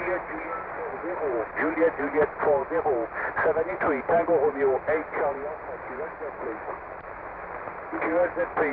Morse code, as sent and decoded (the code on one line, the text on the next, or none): none